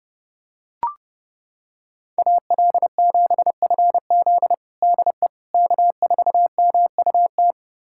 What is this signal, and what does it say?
Morse code, heard as a keyed tone, about 32 wpm, 700 Hz. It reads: AL7FZ DE K4MUT
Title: USA Callsign DE Callsign - 5 Characters - 30wpm